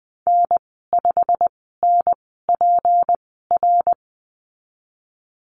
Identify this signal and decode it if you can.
Morse code, heard as a keyed tone, about 19 wpm, spaced as for 14 wpm, 700 Hz. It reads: N5NPR
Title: 186 Lesson 30g - Letter b - Call Signs 20wpm